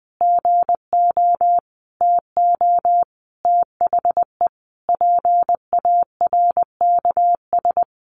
Morse code, heard as a keyed tone, about 20 wpm, 700 Hz. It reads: GO TO THE PARKS
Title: Additional Easy Phrases 20wpm